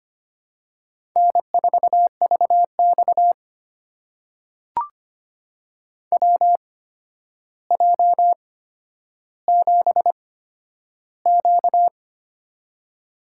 Morse code, heard as a keyed tone, about 25 wpm, spaced as for 11 wpm, 700 Hz. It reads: N4VX WJ7Q
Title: USA Call Signs - 4 Characters - 8x Character Spacing - 25wpm